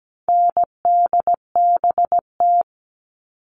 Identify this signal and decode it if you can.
Morse code, heard as a keyed tone, about 17 wpm, 700 Hz. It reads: NDBT
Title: All Cut Numbers - 4-Digit 17wpm